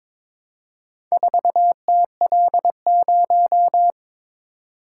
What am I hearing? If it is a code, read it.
Morse code, heard as a keyed tone, about 22 wpm, 700 Hz. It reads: 4TL0
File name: practice / 22wpm / Four Letter-Number Combinations - 22wpm